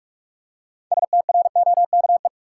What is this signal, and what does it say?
Morse code, heard as a keyed tone, about 42 wpm, 700 Hz. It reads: STROKE
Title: Six Letter Words - 45wpm